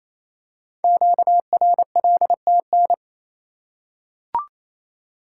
Morse code, heard as a keyed tone, about 29 wpm, 700 Hz. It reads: QRLTN
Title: Code Groups - 5 Letters - 28wpm